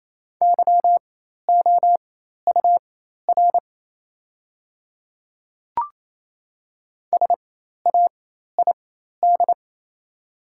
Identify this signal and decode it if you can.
Morse code, heard as a keyed tone, about 28 wpm, spaced as for 13 wpm, 700 Hz. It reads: YOUR SAID
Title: 4x Character Spacing - Top 100 Words - 28wpm